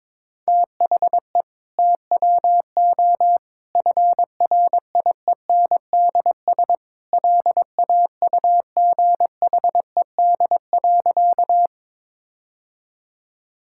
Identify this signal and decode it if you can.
Morse code, heard as a keyed tone, about 22 wpm, 700 Hz. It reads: THE TWO FRIENDS LAUGHED.
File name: practice / 22wpm / Sentences with 2-Word Phrases - 22wpm